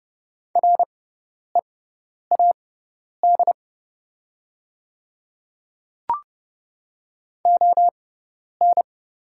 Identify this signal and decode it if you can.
Morse code, heard as a keyed tone, about 30 wpm, spaced as for 10 wpm, 690 Hz. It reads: READ ON
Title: 6x Character Spacing - Top 500 Words - 30wpm